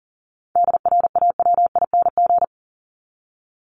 Morse code, heard as a keyed tone, about 40 wpm, 700 Hz. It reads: DRAWING
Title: ING Suffix - Top 100 Words with Base Words 40wpm